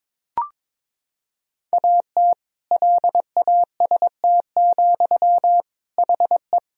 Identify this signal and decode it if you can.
Morse code, heard as a keyed tone, about 22 wpm, 700 Hz. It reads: AT LAST, HE
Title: Sentences from Top 1000 Words - Review Part 3 - 22wpm